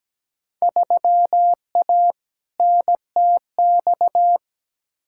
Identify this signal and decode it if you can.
Morse code, heard as a keyed tone, about 17 wpm, 700 Hz. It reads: 3A NTX
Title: Field Day Exchanges - 17wpm